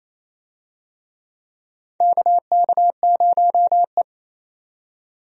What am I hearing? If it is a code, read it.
Morse code, heard as a keyed tone, about 28 wpm, 700 Hz. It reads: KK0E
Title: USA Call Signs - 4 Characters - 28wpm